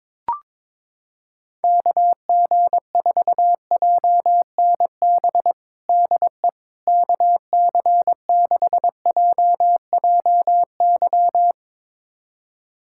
Morse code, heard as a keyed tone, about 21 wpm, 700 Hz. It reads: KG4JNB DE KC6JJY
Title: USA Callsign DE Callsign 22wpm